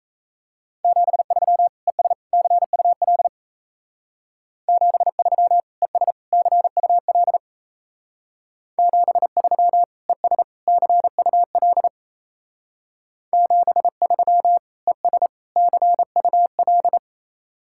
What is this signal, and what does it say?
Morse code, heard as a keyed tone, about 34 wpm, 700 Hz. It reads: 73 ES CUL 73 ES CUL 73 ES CUL 73 ES CUL
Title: Speed-Racing Common QSO Phrases - 28wpm